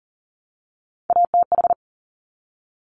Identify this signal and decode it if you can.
Morse code, heard as a keyed tone, about 42 wpm, 700 Hz. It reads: ATH
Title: Top Three Letter Combinations in Words 40wpm